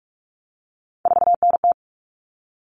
Morse code, heard as a keyed tone, about 44 wpm, 700 Hz. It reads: 4NT
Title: Common Cut Numbers - 3-Digit 45wpm